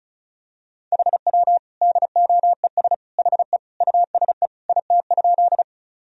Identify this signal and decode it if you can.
Morse code, heard as a keyed tone, about 35 wpm, 700 Hz. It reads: HW DOES HE USE IT?
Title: Sentences from Top 200 Words and Abbreviations 35wpm